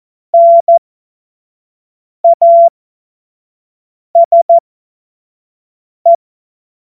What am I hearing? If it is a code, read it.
Morse code, heard as a keyed tone, about 14 wpm, spaced as for 5 wpm, 680 Hz. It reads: NASE
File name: Morsekurs / Witze / Witz 09. November 2021, 25 F70